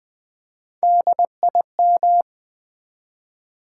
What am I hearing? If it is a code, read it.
Morse code, heard as a keyed tone, about 20 wpm, 700 Hz. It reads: DIM
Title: Build A Word - 1 Letter At A Time - 3 to 5-Letter Words - 20wpm